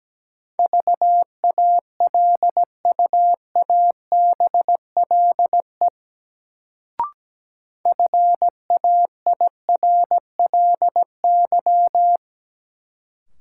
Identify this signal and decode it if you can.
Morse code, heard as a keyed tone, about 17 wpm, 700 Hz. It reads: VALUABLE FAIRLY
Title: Sentences from Top 1100 Words - New Word Drill - 17wpm